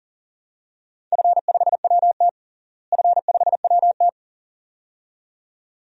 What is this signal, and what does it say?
Morse code, heard as a keyed tone, about 37 wpm, 700 Hz. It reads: F5WT F5WT
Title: DXCC-100 Call Signs - 4 Characters - Repeated 2x - 40wpm